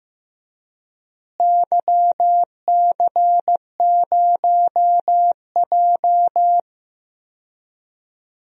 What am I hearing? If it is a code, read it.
Morse code, heard as a keyed tone, about 15 wpm, 700 Hz. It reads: YC0J
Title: DXCC-100 Call Signs - 4 Characters - Repeated 3x - 15wpm